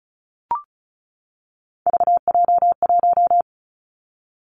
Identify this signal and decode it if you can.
Morse code, heard as a keyed tone, about 36 wpm, 700 Hz. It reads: VJ1
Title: Three Challenging Letter-Number Combinations 35wpm